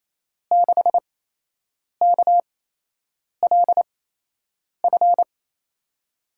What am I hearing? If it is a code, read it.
Morse code, heard as a keyed tone, about 28 wpm, spaced as for 8 wpm, 700 Hz. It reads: 6KLF
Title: USA Call Signs - 8x Character Spacing - 28wpm